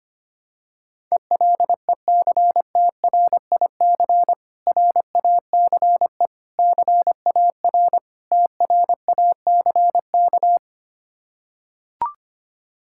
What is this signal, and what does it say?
Morse code, heard as a keyed tone, about 25 wpm, 700 Hz. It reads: ELECTRIC RACE CAR TRACK
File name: practice / 25wpm / Difficult Words with Hints - 25wpm